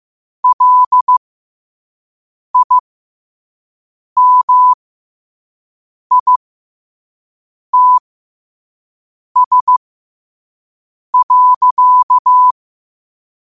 Morse code, heard as a keyed tone, about 15 wpm, spaced as for 5 wpm, 1000 Hz. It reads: LIMITS.
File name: qotdfiles / 5wpm